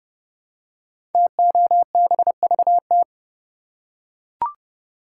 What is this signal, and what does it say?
Morse code, heard as a keyed tone, about 30 wpm, 700 Hz. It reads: TOBVT
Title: Code Groups - 5 Letters - 30wpm